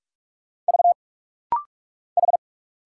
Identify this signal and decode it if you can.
Morse code, heard as a keyed tone, about 44 wpm, 700 Hz. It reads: V H
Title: Single Letter-Number Alt - Rapid-Fire 45wpm